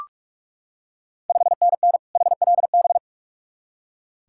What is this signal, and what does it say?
Morse code, heard as a keyed tone, about 46 wpm, 690 Hz. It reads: 5NN HLB
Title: Ontario QSO Party Exchange 45wpm